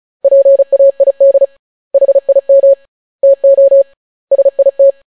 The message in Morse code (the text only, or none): PAID HIM TO SIT